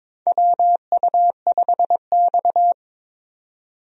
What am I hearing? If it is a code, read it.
Morse code, heard as a keyed tone, about 22 wpm, 700 Hz. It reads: WU5X
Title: USA Call Signs - 4 Characters - 22wpm